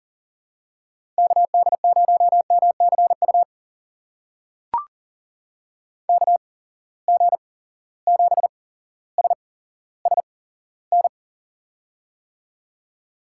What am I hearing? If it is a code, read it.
Morse code, heard as a keyed tone, about 40 wpm, spaced as for 17 wpm, 700 Hz. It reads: KD0MCU KG7SSN
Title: USA Call Signs - 6 Characters - 8x Character Spacing - 40wpm